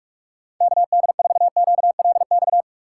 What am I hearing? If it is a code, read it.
Morse code, heard as a keyed tone, about 49 wpm, 700 Hz. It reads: KD4QLX